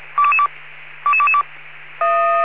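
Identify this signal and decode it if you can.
Five-tone selective call (ZVEI), heard as quick runs of five tones, tones of about 70 ms. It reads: 22222
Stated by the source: ZVEI five-tone selective call received on an SDR.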